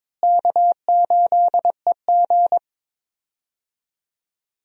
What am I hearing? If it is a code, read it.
Morse code, heard as a keyed tone, about 22 wpm, 700 Hz. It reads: K8EG
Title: USA Call Signs - 4 Characters - Repeated 3x - 22wpm